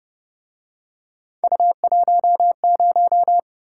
Morse code, heard as a keyed tone, about 31 wpm, 700 Hz. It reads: U10